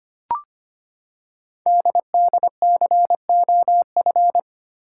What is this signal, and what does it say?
Morse code, heard as a keyed tone, about 25 wpm, 700 Hz. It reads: DDCOF
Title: Code Groups - 5 Letters - 25wpm